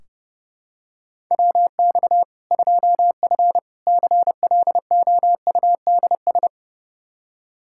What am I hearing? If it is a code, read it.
Morse code, heard as a keyed tone, about 32 wpm, 700 Hz. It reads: WX 2F CLOUDS
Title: Practice QSO - Exchange 2 - Info - 30wpm